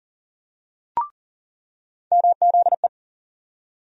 Morse code, heard as a keyed tone, about 40 wpm, 700 Hz. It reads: MZE
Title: USA Call Sign Postfixes - 3 Letters 40wpm